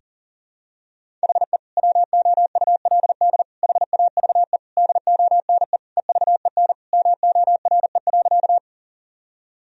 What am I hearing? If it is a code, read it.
Morse code, heard as a keyed tone, about 40 wpm, 700 Hz. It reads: HE WOULD HAVE DONE EVEN MORE.
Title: Sentences from Top 1000 Words - Review Part 1 - 40wpm